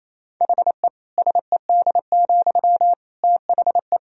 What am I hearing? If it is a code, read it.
Morse code, heard as a keyed tone, about 27 wpm, 700 Hz. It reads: HE SED, THE
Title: Sentences from Top 400 Words and Abbreviations 28wpm